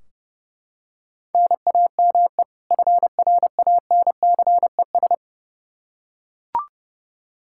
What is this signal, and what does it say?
Morse code, heard as a keyed tone, about 30 wpm, 700 Hz. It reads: NAME FRANCES
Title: Practice QSO - Exchange 1 - Beginning - 30wpm